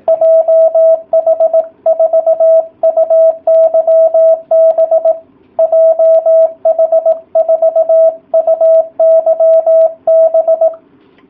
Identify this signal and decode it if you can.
Morse code, heard as a keyed tone, about 18 wpm, 660 Hz. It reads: JH4UYB JH4UYB